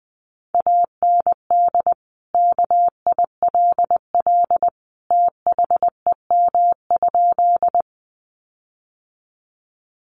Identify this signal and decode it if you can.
Morse code, heard as a keyed tone, about 20 wpm, 700 Hz. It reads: AND KILL THEM?
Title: Sentences from Top 1400 Words 20wpm